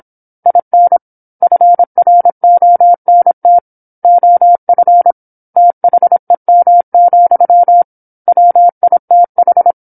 Morse code, heard as a keyed tone, about 27 wpm, 700 Hz. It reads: IN FRONT OF THEM, WITH
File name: 08-Chapter-08 26 WPM